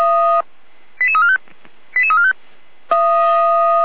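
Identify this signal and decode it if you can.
Five-tone selective call (ZVEI), heard as quick runs of five tones, tones of about 70 ms. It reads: 88345 88345
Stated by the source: ZVEI five-tone selective call received by SDR.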